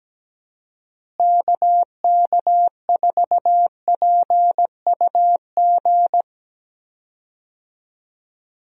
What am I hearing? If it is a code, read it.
Morse code, heard as a keyed tone, about 17 wpm, 700 Hz. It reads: KK4PUG
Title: USA Call Signs - 6 Characters - Repeated 2x - 17wpm